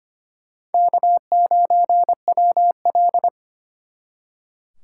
Morse code, heard as a keyed tone, about 33 wpm, 700 Hz. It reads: K9WL